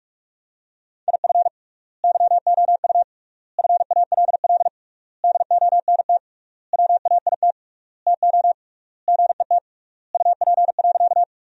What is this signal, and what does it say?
Morse code, heard as a keyed tone, about 46 wpm, 700 Hz. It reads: IF YOU FALL DONT WAIT TO GET UP.
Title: 3x Word-Spacing - Sentences from Top 500 Words - 45wpm